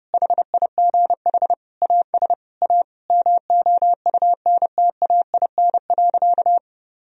Morse code, heard as a keyed tone, about 30 wpm, 700 Hz. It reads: HIGH AS A MOUNTAIN.
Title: Sentences from Top 1000 Words - Review Part 1 - 30wpm